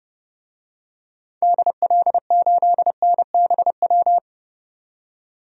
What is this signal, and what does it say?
Morse code, heard as a keyed tone, about 30 wpm, 700 Hz. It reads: DL8NBW
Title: DXCC-100 Call Signs - 6 Characters - 30wpm